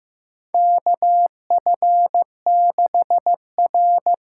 Morse code, heard as a keyed tone, about 15 wpm, 700 Hz. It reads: KF6R
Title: USA Call Signs - 6 Characters - 8x Character Spacing - 15wpm